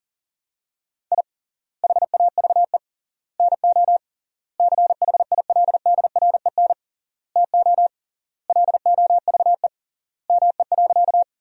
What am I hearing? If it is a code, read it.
Morse code, heard as a keyed tone, about 40 wpm, 700 Hz. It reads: I HAVE NO CHILDREN TO LOVE ME.
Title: 3x Word-Spacing - Sentences from Top 400 Words - 40wpm